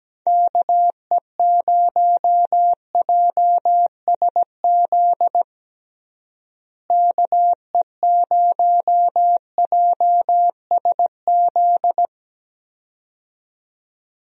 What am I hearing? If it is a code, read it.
Morse code, heard as a keyed tone, about 17 wpm, 700 Hz. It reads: KE0JSZ KE0JSZ
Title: USA Call Signs - 6 Characters - Repeated 2x - 17wpm